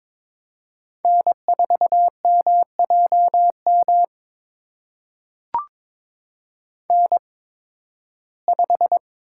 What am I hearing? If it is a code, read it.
Morse code, heard as a keyed tone, about 22 wpm, spaced as for 14 wpm, 700 Hz. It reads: N4MJM N5
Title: USA Call Signs - 8x Character Spacing - 22wpm